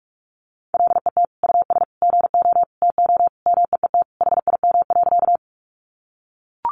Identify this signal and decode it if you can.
Morse code, heard as a keyed tone, about 44 wpm, 700 Hz. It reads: LET US GO TO MEET HIM.
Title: Sentences from Top 1100 Words 45wpm